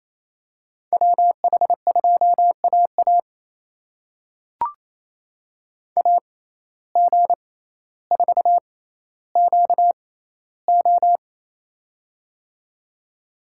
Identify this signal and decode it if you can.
Morse code, heard as a keyed tone, about 28 wpm, spaced as for 14 wpm, 700 Hz. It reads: WH2AA AG4QO
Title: USA Call Signs - 5 Characters - 6x Character Spacing - 28wpm